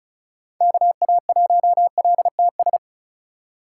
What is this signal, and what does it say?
Morse code, heard as a keyed tone, about 36 wpm, 700 Hz. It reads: KA1LTS
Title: USA Call Signs - 2x Character Spacing - 35wpm